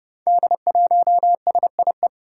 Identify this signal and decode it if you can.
Morse code, heard as a keyed tone, about 30 wpm, 700 Hz. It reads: D1SIE
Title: Five Challenging Letter-Number Combinations 30wpm